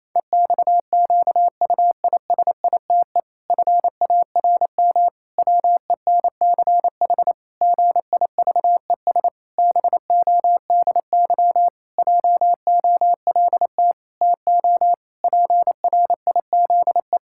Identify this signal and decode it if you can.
Morse code, heard as a keyed tone, about 28 wpm, 700 Hz. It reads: EXQUISITE FARM WENCH GIVES BODY JOLT TO PRIZE
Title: Pangram Challenge v2 - 28wpm